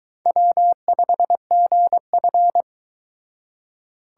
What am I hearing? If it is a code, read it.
Morse code, heard as a keyed tone, about 24 wpm, 700 Hz. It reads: W5GF